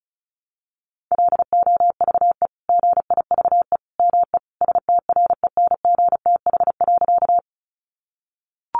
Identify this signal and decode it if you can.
Morse code, heard as a keyed tone, about 34 wpm, 700 Hz. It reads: LOVE GIVE ME STRENGTH.
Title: Sentences from Top 1300 Words 35wpm